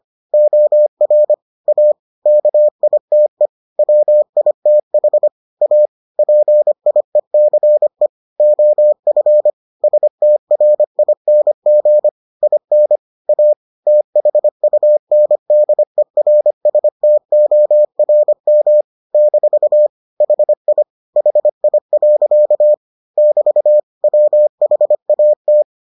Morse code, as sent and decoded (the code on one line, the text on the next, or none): --- .-. / .- / -.- .. - . / .-- .. - .... / .- / .--. .. -.-. / --- ..-. / ... - .-. .. -. --. / .. -. / .- / - .... ..- -. -.. .-. ... - --- .-. -- / -....- / .... .. / .... .. .-.-.- / -...- / .-- .... .- -
OR A KITE WITH A PIECE OF STRING IN A THUNDERSTORM - HI HI. <BT> WHAT